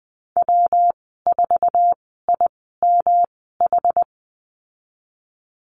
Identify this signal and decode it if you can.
Morse code, heard as a keyed tone, about 20 wpm, spaced as for 15 wpm, 700 Hz. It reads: W4IMH
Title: 122 Lesson 20g - Number 3 - Call Signs 20wpm